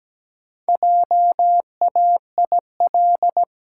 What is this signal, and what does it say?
Morse code, heard as a keyed tone, about 17 wpm, 700 Hz. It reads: JAIL
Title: Pangram Challenge v2 - 17wpm